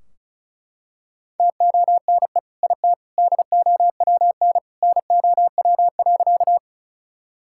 Morse code, heard as a keyed tone, about 34 wpm, 700 Hz. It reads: TONE IT DOWN NOW.
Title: Sentences from Top 1600 Words 35wpm